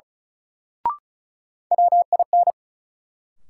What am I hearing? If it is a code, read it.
Morse code, heard as a keyed tone, about 36 wpm, 700 Hz. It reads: WIN
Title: Sentences from Top 1500 Words - New Word Drill - 35wpm